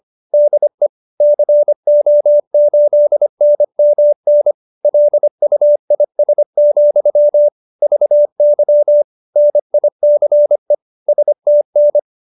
Morse code, heard as a keyed tone, about 25 wpm, 590 Hz. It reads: DE CO8NMN LUIS, VY NICE STN